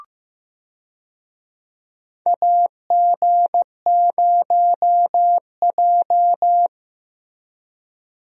Morse code, heard as a keyed tone, about 16 wpm, 700 Hz. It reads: AG0J
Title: USA Call Signs - 4 Characters - Repeated 2x - 15wpm